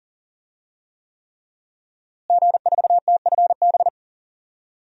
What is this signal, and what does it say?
Morse code, heard as a keyed tone, about 38 wpm, 700 Hz. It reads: G4TFB
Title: DXCC-100 Call Signs - 5 Characters - Repeated 3x - 40wpm